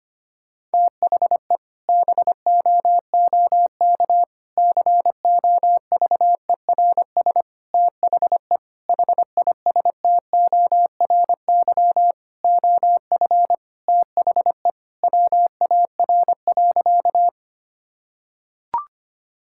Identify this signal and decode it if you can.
Morse code, heard as a keyed tone, about 25 wpm, 700 Hz. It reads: THE BOOK COVERS THE HISTORY OF THE WAR.
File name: practice / 25wpm / Sentences with 4-Word Phrases - 25wpm